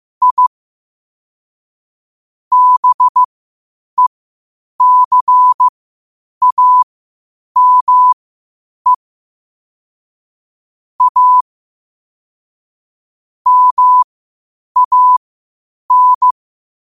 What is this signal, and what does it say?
Morse code, heard as a keyed tone, about 15 wpm, spaced as for 8 wpm, 1000 Hz. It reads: I BECAME A MAN
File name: qotdfiles / 8wpm